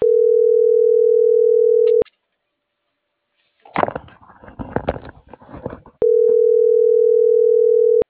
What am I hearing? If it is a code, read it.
Ringback tone (North America), 440 + 480 Hz.